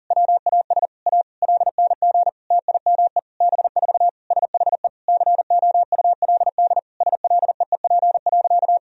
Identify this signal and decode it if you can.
Morse code, heard as a keyed tone, about 40 wpm, 700 Hz. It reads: WAS A LNG TIME B4 SHE COULD SLEEP.
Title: Sentences from Top 500 Words and Abbreviations 40wpm